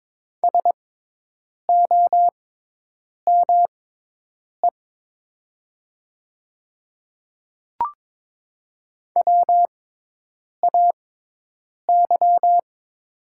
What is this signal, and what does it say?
Morse code, heard as a keyed tone, about 22 wpm, spaced as for 8 wpm, 700 Hz. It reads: SOME WAY
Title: 6x Character Spacing - Top 100 Words - 22wpm